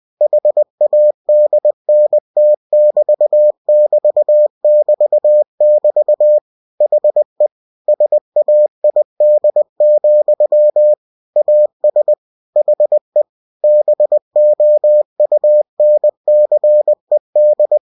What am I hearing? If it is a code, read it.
Morse code, heard as a keyed tone, about 20 wpm, 600 Hz. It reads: HADNT<BT><BT><BT><BT> HE SAID, AS HE BOUNCED
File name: cw-books / winnie-the-pooh / winnie-20wpm-page-0003